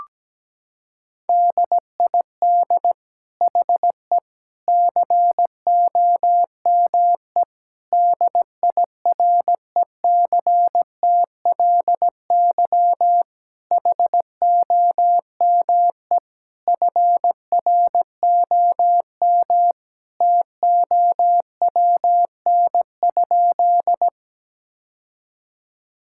Morse code, heard as a keyed tone, about 17 wpm, 700 Hz. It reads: DID HE COME DIRECTLY HOME FROM TOWN?
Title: Sentences from Top 1000 Words - Review Part 5 - 17wpm